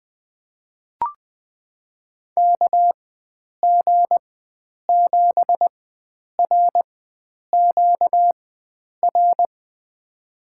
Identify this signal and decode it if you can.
Morse code, heard as a keyed tone, about 20 wpm, spaced as for 9 wpm, 700 Hz. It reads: KG7RQR
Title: USA Call Signs - 4x Character Spacing - 20wpm